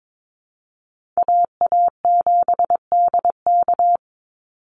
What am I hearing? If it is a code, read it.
Morse code, heard as a keyed tone, about 22 wpm, 700 Hz. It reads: AA7DK
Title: NA SOTA Chaser Callsigns 22wpm